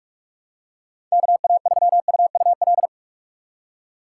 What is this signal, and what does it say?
Morse code, heard as a keyed tone, about 46 wpm, 700 Hz. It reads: KA3UUL
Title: USA Call Signs - 6 Characters - 3x Character Spacing - 45wpm